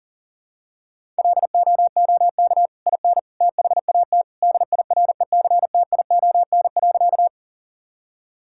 Morse code, heard as a keyed tone, about 40 wpm, 700 Hz. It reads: LOOK IN THAT DIRECTION.
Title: Sentences from Top 1000 Words - Review Part 5 - 40wpm